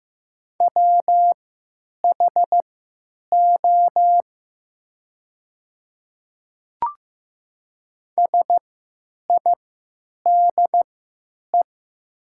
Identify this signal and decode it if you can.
Morse code, heard as a keyed tone, about 15 wpm, spaced as for 9 wpm, 700 Hz. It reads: WHO SIDE
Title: 3x Character Spacing - Top 200 Words - 15wpm